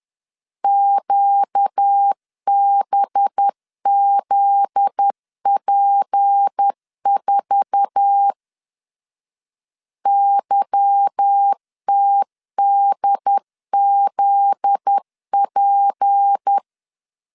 Morse code, heard as a keyed tone, about 10 wpm, 800 Hz. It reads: QBZP4 YTDZP
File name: LEZIONI / 8 YJ45 / 8 YJ4 10wpm 2633